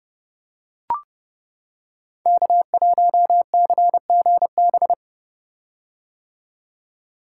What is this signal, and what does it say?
Morse code, heard as a keyed tone, about 30 wpm, 700 Hz. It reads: K1CGB
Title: Code Groups - 5 Letter:Numbers - 30wpm